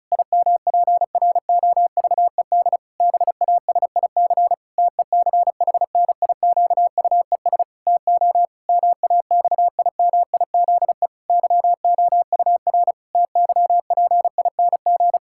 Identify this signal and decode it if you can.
Morse code, heard as a keyed tone, about 36 wpm, 700 Hz. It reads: IMPROVED BASIC TECHNIQUES TO MAXIMIZE YOUR TYPING
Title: Pangram Challenge v2 - 35wpm